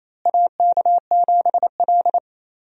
Morse code, heard as a keyed tone, about 27 wpm, 700 Hz. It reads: AK7L
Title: USA Call Signs - 4 Characters - Repeated 3x - 28wpm